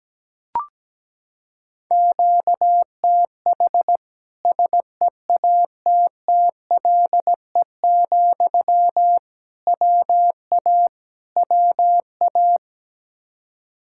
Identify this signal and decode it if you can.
Morse code, heard as a keyed tone, about 17 wpm, 700 Hz. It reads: QTH SEATTLE, WA WA
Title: Practice QSO - Exchange 1 - Beginning - 17wpm